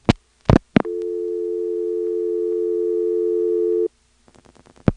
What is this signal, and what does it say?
Dial tone (North America), 350 + 440 Hz.